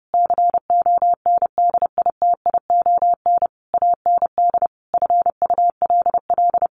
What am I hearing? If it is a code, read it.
Morse code, heard as a keyed tone, about 30 wpm, 700 Hz. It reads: CONDITION AND FULL
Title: Sentences from Top 1500 Words 30wpm